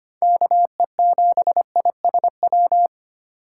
Morse code, heard as a keyed tone, about 25 wpm, 700 Hz. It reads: KE7ISW
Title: USA Call Signs - 6 Characters - 4x Character Spacing - 25wpm